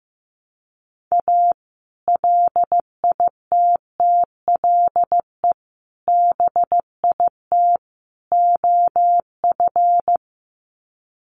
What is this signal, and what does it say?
Morse code, heard as a keyed tone, about 15 wpm, 700 Hz. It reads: A LITTLE BIT OF
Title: Sets of 4 Words 15wpm